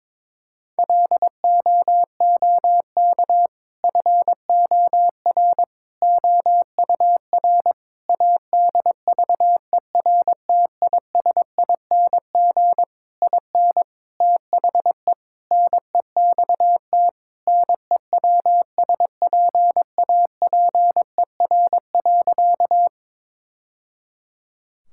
Morse code, heard as a keyed tone, about 22 wpm, 700 Hz. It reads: LOOK FOR OUR ADVERTISING IN THE NEXT NEWSPAPER.
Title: Sentences from Top 1000 Words - Review Part 6 - 22wpm